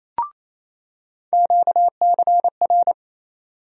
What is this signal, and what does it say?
Morse code, heard as a keyed tone, about 28 wpm, 700 Hz. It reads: QCR
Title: USA Call Sign Postfixes - 3 Letters 28wpm